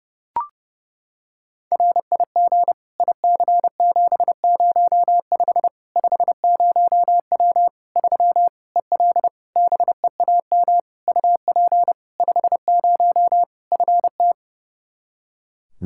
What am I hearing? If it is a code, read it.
Morse code, heard as a keyed tone, about 30 wpm, 700 Hz. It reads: RIG IC705 50W 3 EL BEAM UP 50 FT
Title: Practice QSO - Exchange 2 - Info - 30wpm